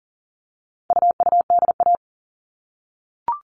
Morse code, heard as a keyed tone, about 40 wpm, 700 Hz. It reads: UUDA
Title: All Cut Numbers - 4-Digit 40wpm